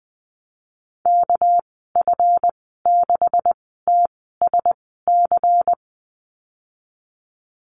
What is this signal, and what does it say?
Morse code, heard as a keyed tone, about 20 wpm, spaced as for 15 wpm, 700 Hz. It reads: KF6TSC
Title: 200 Lesson 32g - Letter k - Call Signs 20wpm